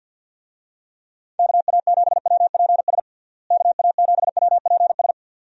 Morse code, heard as a keyed tone, about 51 wpm, 700 Hz. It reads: KA7WPS KA7WPS
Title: USA Call Signs - 6 Characters - Repeated 2x - 50wpm